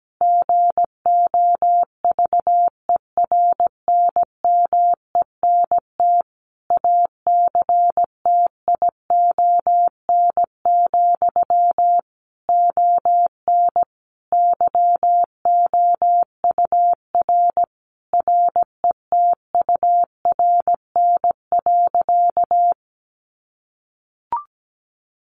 Morse code, heard as a keyed tone, about 17 wpm, 700 Hz. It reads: GOVERNMENT ACTION, ON YOUR RETURN.
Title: A Sound Of Thunder 17wpm